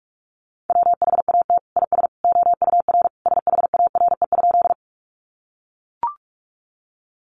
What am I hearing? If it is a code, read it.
Morse code, heard as a keyed tone, about 44 wpm, 700 Hz. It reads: WHAT IS OUR SHARE?
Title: Sentences from Top 900 Words 45wpm